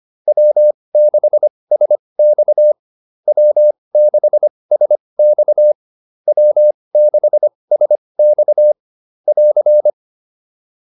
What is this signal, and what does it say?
Morse code, heard as a keyed tone, about 25 wpm, spaced as for 20 wpm, 600 Hz. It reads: W6SX W6SX W6SX <AR>